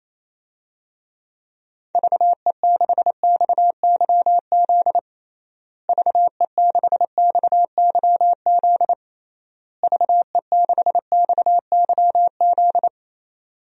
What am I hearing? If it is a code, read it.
Morse code, heard as a keyed tone, about 27 wpm, 700 Hz. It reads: VE6XYZ VE6XYZ VE6XYZ
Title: DXCC-100 Call Signs - 6 Characters - Repeated 3x - 28wpm